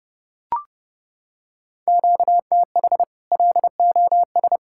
Morse code, heard as a keyed tone, about 30 wpm, 700 Hz. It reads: QTH LOS
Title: Practice QSO - Exchange 1 - Beginning - 30wpm